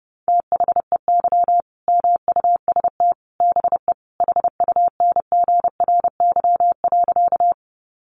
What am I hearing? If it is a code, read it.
Morse code, heard as a keyed tone, about 30 wpm, 700 Hz. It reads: THEY MUST BE HUNGRY.
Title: Sentences from Top 1400 Words 30wpm